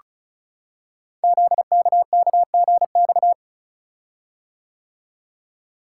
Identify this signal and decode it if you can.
Morse code, heard as a keyed tone, about 33 wpm, 700 Hz. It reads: ZKKGX
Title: Code Groups - 5 Letter:Numbers - 35wpm